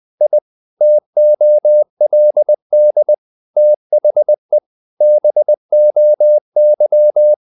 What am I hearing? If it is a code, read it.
Morse code, heard as a keyed tone, about 20 wpm, 600 Hz. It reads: I TOLD THE BOY
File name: the old man and the sea / the old man and the sea part 1 20wpm